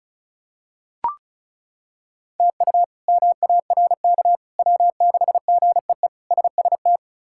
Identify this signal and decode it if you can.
Morse code, heard as a keyed tone, about 34 wpm, 700 Hz. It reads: TU MARK W6GEE SST
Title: K1USN SST QSO Practice - 35wpm